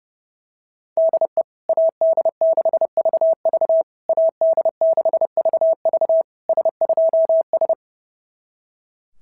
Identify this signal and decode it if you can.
Morse code, heard as a keyed tone, about 28 wpm, 650 Hz. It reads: DE AD6VV AD6VV S2S
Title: SOTA QSO Practice 30wpm